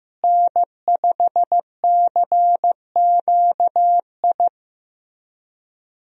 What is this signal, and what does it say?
Morse code, heard as a keyed tone, about 15 wpm, 700 Hz. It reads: N5CQI
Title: USA Call Signs - 5 Characters - Repeated 3x - 15wpm